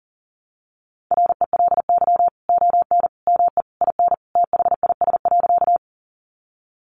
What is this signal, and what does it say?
Morse code, heard as a keyed tone, about 37 wpm, 700 Hz. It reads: RELY ON ME IN THIS.
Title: Sentences from Top 1200 Words 40wpm